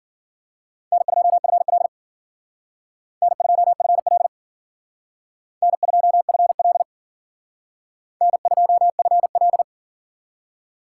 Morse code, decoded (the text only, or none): N2FL N2FL N2FL N2FL